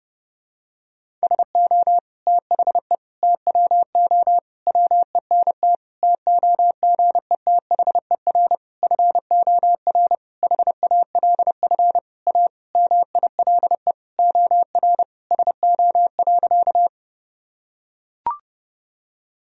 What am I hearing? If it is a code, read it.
Morse code, heard as a keyed tone, about 32 wpm, 700 Hz. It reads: SO THE TWO WENT TOGETHER FOR HALF A MILE OR SO.
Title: Sentences from Top 1000 Words - Review Part 1 - 30wpm